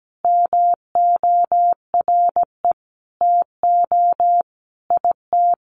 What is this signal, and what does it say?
Morse code, heard as a keyed tone, about 17 wpm, 700 Hz. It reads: MORE TO IT
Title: Sentences from Top 100 Words 17wpm